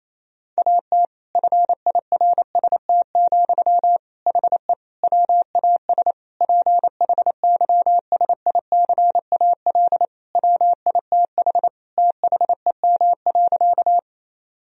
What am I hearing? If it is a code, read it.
Morse code, heard as a keyed tone, about 28 wpm, 700 Hz. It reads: AT FIRST, HE WAS PHYSICAL WITH THEM.
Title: Sentences from Top 1000 Words - Review Part 4 - 28wpm